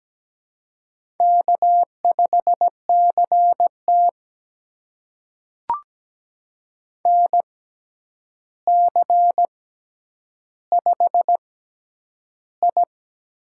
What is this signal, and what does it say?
Morse code, heard as a keyed tone, about 17 wpm, spaced as for 9 wpm, 700 Hz. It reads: K5CT NC5I
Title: USA Call Signs - 4 Characters - 6x Character Spacing - 17wpm